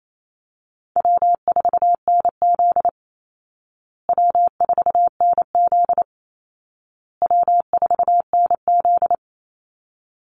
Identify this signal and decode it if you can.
Morse code, heard as a keyed tone, about 28 wpm, 700 Hz. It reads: W4NZ W4NZ W4NZ
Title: Easy Call Signs Repeated 3x - 28wpm